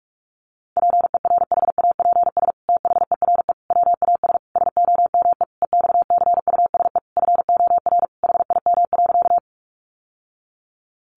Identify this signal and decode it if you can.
Morse code, heard as a keyed tone, about 44 wpm, 700 Hz. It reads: PERHAPS THERE WAS SOME EXCUSE FOR HIM.
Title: Sentences from Top 700 Words 45wpm